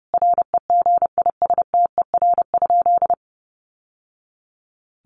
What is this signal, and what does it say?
Morse code, heard as a keyed tone, about 30 wpm, 700 Hz. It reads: REGISTER?
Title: Sentences from Top 1200 Words 30wpm